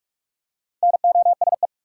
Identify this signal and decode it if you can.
Morse code, heard as a keyed tone, about 45 wpm, 700 Hz. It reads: NOSE
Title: Four Letter Words - 45wpm